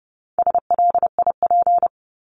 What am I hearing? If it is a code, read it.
Morse code, heard as a keyed tone, about 30 wpm, 700 Hz. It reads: SLIP
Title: Top 1751-2000 Words 30wpm